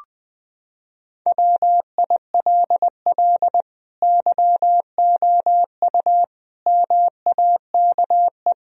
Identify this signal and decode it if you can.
Morse code, heard as a keyed tone, about 21 wpm, 700 Hz. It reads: WILL YOU MAKE
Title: Sentences from Top 1000 Words - Review Part 3 - 20wpm